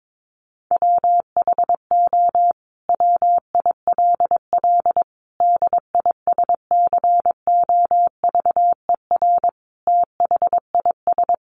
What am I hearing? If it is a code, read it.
Morse code, heard as a keyed tone, about 22 wpm, 700 Hz. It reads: WHO WILL DISCOVER THIS
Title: Sentences from Top 1100 Words 22wpm